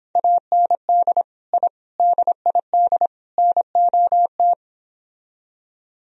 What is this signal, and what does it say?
Morse code, heard as a keyed tone, about 26 wpm, 700 Hz. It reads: AND I DID NOT